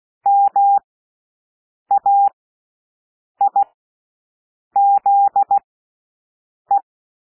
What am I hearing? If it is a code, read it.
Morse code, heard as a keyed tone, about 16 wpm, spaced as for 6 wpm, 800 Hz. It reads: MAIZE
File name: cwmp3 / W1 SOUND WORDS 16 wpm